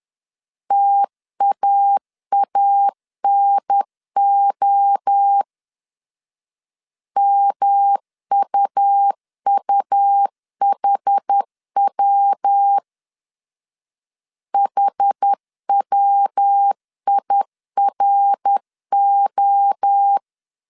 Morse code, heard as a keyed tone, about 10 wpm, 800 Hz. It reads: TAANO MUUHW HWIRO
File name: CW / DET / 1 3 1500C S B 10wpm 3335